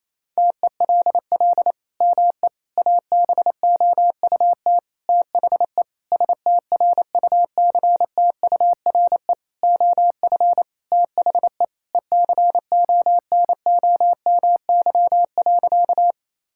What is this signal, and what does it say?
Morse code, heard as a keyed tone, about 28 wpm, 700 Hz. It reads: TELL ME ABOUT THE STRUCTURE OF THE ECONOMY.
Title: Sentences from Top 1000 Words - Review Part 6 - 28wpm